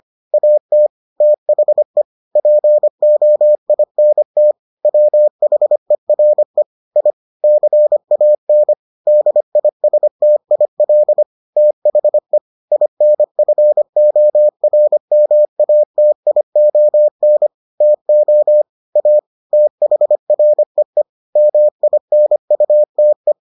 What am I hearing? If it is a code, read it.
Morse code, heard as a keyed tone, about 25 wpm, 600 Hz. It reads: AT THE POINT WHERE I CAN DISTIL THE INFORMATION TO A THREE MINUTE